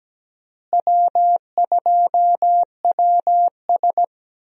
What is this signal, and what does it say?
Morse code, heard as a keyed tone, about 17 wpm, 700 Hz. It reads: W2WS
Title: USA Call Signs - 4 Characters - Repeated 3x - 17wpm